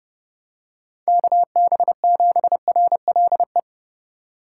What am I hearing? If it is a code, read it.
Morse code, heard as a keyed tone, about 30 wpm, 700 Hz. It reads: KB7RLE
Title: USA Call Signs - 6 Characters - 4x Character Spacing - 30wpm